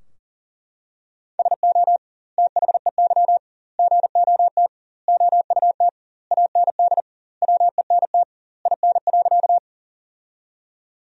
Morse code, heard as a keyed tone, about 40 wpm, 700 Hz. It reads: SO THEY GOT OUT AND WENT IN.
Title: 2x Word-Spacing - Sentences from Top 300 Words - 40wpm